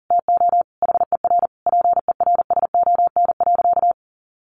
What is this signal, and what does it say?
Morse code, heard as a keyed tone, about 40 wpm, 700 Hz. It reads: TO HER PERSON.
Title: Sentences from Top 1400 Words 40wpm